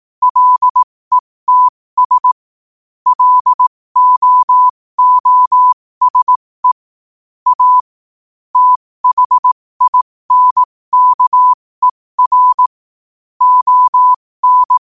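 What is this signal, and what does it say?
Morse code, heard as a keyed tone, about 18 wpm, spaced as for 15 wpm, 1000 Hz. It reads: LETS LOOSE A THINKER ON
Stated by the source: Morse practice sent at 15 wpm.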